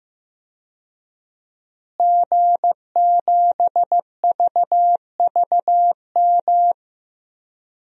Morse code, heard as a keyed tone, about 15 wpm, 700 Hz. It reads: G7VVM
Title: DXCC-100 Call Signs - 5 Characters - Repeated 3x - 15wpm